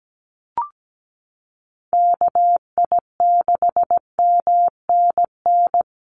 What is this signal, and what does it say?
Morse code, heard as a keyed tone, about 17 wpm, 700 Hz. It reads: KI6MNN
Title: USA Call Signs Repeated 2x 17wpm